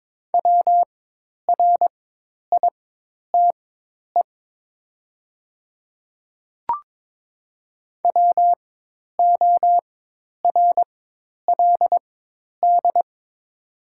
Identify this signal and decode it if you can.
Morse code, heard as a keyed tone, about 22 wpm, spaced as for 10 wpm, 700 Hz. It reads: WRITE WORLD
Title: 4x Character Spacing - Top 200 Words - 22wpm